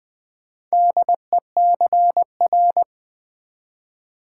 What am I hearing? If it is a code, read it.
Morse code, heard as a keyed tone, about 20 wpm, 700 Hz. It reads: DECR
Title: Build A Word - 2 Letters At A Time - 8-Letter Words - 20wpm